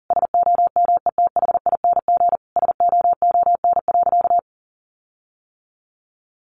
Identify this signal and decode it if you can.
Morse code, heard as a keyed tone, about 40 wpm, 700 Hz. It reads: SOMETHING SOON.
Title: Sentences from Top 1500 Words 40wpm